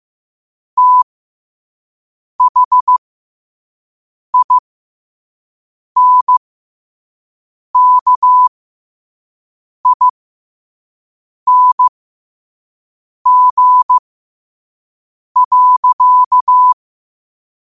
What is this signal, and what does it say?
Morse code, heard as a keyed tone, about 15 wpm, spaced as for 5 wpm, 1000 Hz. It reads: THINKING.
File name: qotdfiles / 5wpm